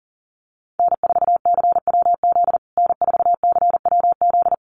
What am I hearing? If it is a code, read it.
Morse code, heard as a keyed tone, about 40 wpm, 700 Hz. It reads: N4CWZ N4CWZ
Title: USA Call Signs Repeated 3x 40wpm